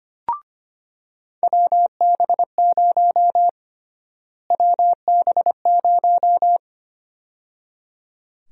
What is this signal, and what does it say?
Morse code, heard as a keyed tone, about 25 wpm, 700 Hz. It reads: WB0 WB0
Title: USA Call Sign Prefixes - Repeated 2x - 25wpm